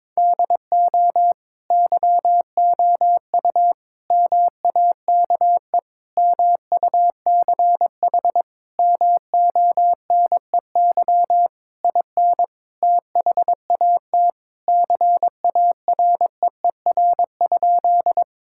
Morse code, heard as a keyed tone, about 22 wpm, 700 Hz. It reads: DO YOU MAKE MUCH MONEY IN THAT CAREER?
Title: Sentences from Top 1000 Words - Review Part 6 - 22wpm